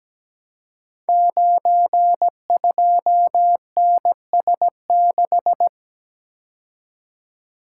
Morse code, heard as a keyed tone, about 17 wpm, 700 Hz. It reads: 92NS6
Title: Code Groups - 5 Letter:Numbers - 17wpm